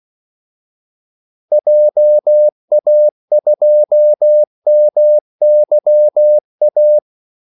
Morse code, heard as a keyed tone, about 16 wpm, 600 Hz. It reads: JA2MYA
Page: Kaldesignaler 16 wpm